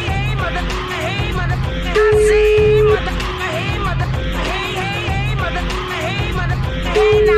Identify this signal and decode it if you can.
Ringback tone (Europe, 425 Hz), with speech.